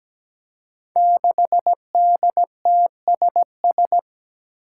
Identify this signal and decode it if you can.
Morse code, heard as a keyed tone, about 17 wpm, 700 Hz. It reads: 6DTSS
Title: Five Challenging Letter-Number Combinations 17wpm